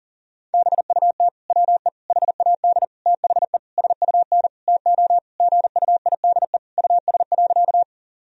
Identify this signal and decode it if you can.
Morse code, heard as a keyed tone, about 40 wpm, 700 Hz. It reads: BUT WE HAD THE SUN TO GUIDE US.
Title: Sentences from Top 1000 Words - Review Part 3 - 40wpm